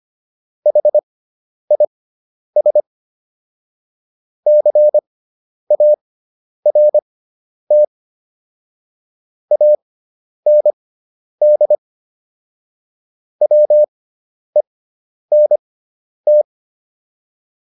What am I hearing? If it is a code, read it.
Morse code, heard as a keyed tone, about 25 wpm, spaced as for 10 wpm, 600 Hz. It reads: HIS CART AND WENT